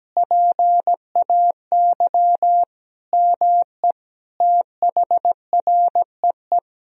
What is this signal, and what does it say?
Morse code, heard as a keyed tone, about 17 wpm, 700 Hz. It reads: PAY ME THREE
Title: Sentences from Top 1000 Words - Review Part 2 - 17wpm